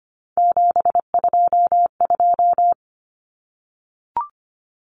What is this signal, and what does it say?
Morse code, heard as a keyed tone, about 25 wpm, 690 Hz. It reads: 722
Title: Numbers - 3-Digit 25wpm